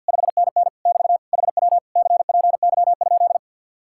Morse code, heard as a keyed tone, about 65 wpm, spaced as for 57 wpm, 700 Hz. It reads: 5NN <BT> HW CPY?
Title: Speed-Racing Common QSO Phrases - 50wpm